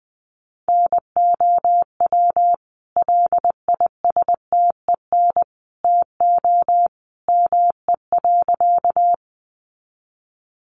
Morse code, heard as a keyed tone, about 20 wpm, 700 Hz. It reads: NOW LISTEN TO ME.
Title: Sentences from Top 400 Words 20wpm